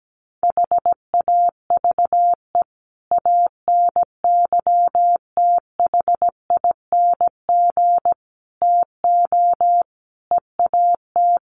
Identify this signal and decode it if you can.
Morse code, heard as a keyed tone, about 17 wpm, 700 Hz. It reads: HAVE ANYTHING TO EAT
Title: Sentences from Top 1500 Words 17wpm